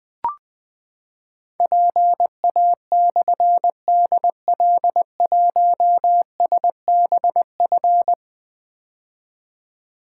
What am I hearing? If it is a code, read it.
Morse code, heard as a keyed tone, about 20 wpm, 700 Hz. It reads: PA/DL1SBF
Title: Difficult Call Signs 20wpm